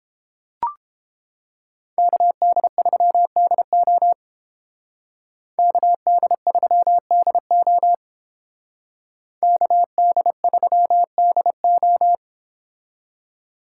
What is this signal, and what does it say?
Morse code, heard as a keyed tone, about 29 wpm, 700 Hz. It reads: KD3DO KD3DO KD3DO